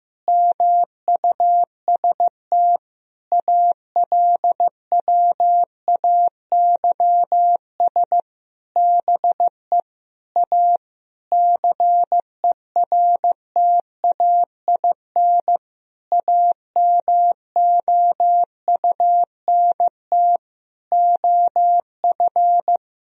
Morse code, heard as a keyed tone, about 15 wpm, 700 Hz. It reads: MUST ALWAYS BE A CERTAIN AMOUNT OF
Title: Sentences from Top 1000 Words - Review Part 4 - 15wpm